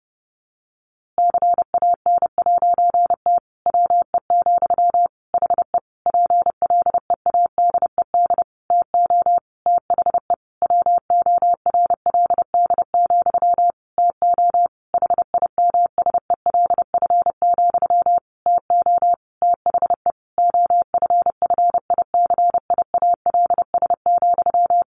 Morse code, heard as a keyed tone, about 30 wpm, 700 Hz. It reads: CAN'T WE, HE PLEADED TO THE WORLD, TO HIMSELF, TO THE OFFICIALS,
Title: A Sound Of Thunder 30wpm